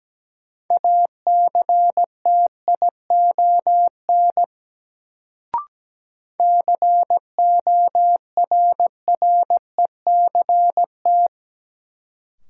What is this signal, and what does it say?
Morse code, heard as a keyed tone, about 17 wpm, 700 Hz. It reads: ACTION CORRECT
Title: Sentences from Top 700 Words - New Word Drill - 17wpm